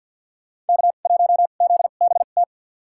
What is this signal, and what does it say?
Morse code, heard as a keyed tone, about 52 wpm, spaced as for 37 wpm, 700 Hz. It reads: K1ZBT
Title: USA Call Signs - 2x Character Spacing - 50wpm